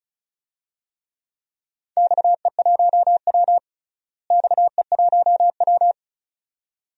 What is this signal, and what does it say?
Morse code, heard as a keyed tone, about 36 wpm, 700 Hz. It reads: XE1W XE1W
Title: DXCC-100 Call Signs - 4 Characters - Repeated 2x - 35wpm